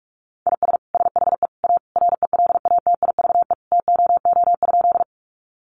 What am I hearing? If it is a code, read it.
Morse code, heard as a keyed tone, about 44 wpm, 700 Hz. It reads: IS SHE A RELATIVE TOO?
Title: Sentences from Top 1400 Words 45wpm